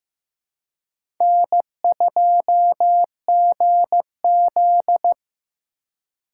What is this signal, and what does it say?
Morse code, heard as a keyed tone, about 15 wpm, 700 Hz. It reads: N2GZ
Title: USA Call Signs - 4 Characters - 2x Character Spacing - 15wpm